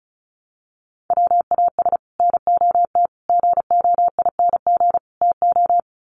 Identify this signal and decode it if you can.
Morse code, heard as a keyed tone, about 34 wpm, 700 Hz. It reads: WAS NOT GOING TO
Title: Sets of 4 Words 35wpm